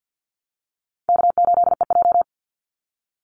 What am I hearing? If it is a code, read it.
Morse code, heard as a keyed tone, about 49 wpm, 700 Hz. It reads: K8EJ
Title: Easy Call Signs Repeated 2x - 50wpm